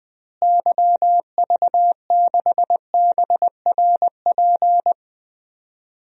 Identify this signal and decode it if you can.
Morse code, heard as a keyed tone, about 20 wpm, 700 Hz. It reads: YV6BRP
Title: DXCC-100 Call Signs - 6 Characters - Repeated 2x - 20wpm